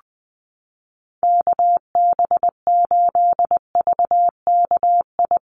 Morse code, heard as a keyed tone, about 20 wpm, 700 Hz. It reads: KB8VKI
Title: USA Call Signs Repeated 3x 20wpm